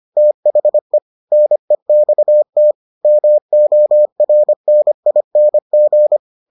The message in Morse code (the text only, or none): THE NEXT MORNING